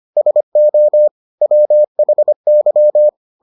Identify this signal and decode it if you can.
Morse code, heard as a keyed tone, about 25 wpm, 600 Hz. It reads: SO WHY